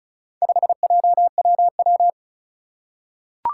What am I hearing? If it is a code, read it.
Morse code, heard as a keyed tone, about 36 wpm, 700 Hz. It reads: 5JWW
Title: Four Challenging Letter-Number Combinations 35wpm